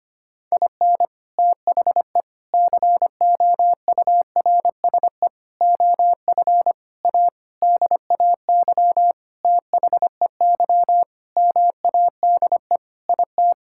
Morse code, heard as a keyed tone, about 25 wpm, 700 Hz. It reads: IN THE COURSE OF A DAY THEY MADE IT
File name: practice / 25wpm / Sentences with 4-Word Phrases - 25wpm